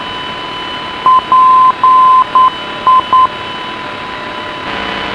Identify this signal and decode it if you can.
Morse code, heard as a keyed tone, about 9 wpm, 1030 Hz, with radio noise. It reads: PI